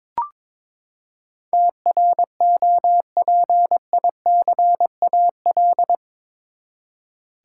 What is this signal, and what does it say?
Morse code, heard as a keyed tone, about 21 wpm, 700 Hz. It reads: TROPICAL
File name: practice / 22wpm / Eight Letter Words - 22wpm